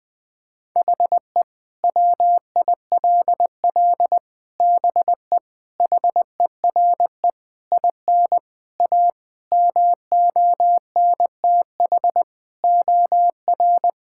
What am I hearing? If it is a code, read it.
Morse code, heard as a keyed tone, about 20 wpm, 700 Hz. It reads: HE WILL BE HERE IN A MONTH OR
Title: Sentences from Top 1000 Words - Review Part 5 - 20wpm